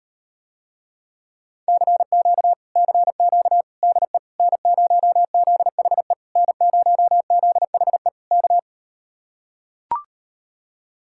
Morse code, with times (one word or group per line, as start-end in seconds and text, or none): 1.7-2.5 CQ
2.8-3.6 CQ
3.8-4.2 DE
4.4-6.1 N0ZHE
6.4-8.1 N0ZHE
8.3-8.6 K